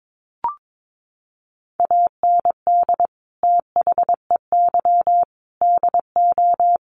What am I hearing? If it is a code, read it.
Morse code, heard as a keyed tone, about 22 wpm, 700 Hz. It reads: AND THEY DO
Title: Sets of 5 Words Encore2 22wpm